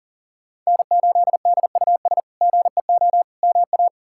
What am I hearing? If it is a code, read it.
Morse code, heard as a keyed tone, about 40 wpm, 700 Hz. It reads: N8DUS GEO MA
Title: K1USN SST QSO Practice - 40wpm